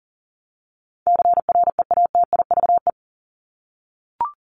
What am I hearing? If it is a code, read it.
Morse code, heard as a keyed tone, about 40 wpm, 700 Hz. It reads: CREATIVE
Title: Top 751-1000 Words 40wpm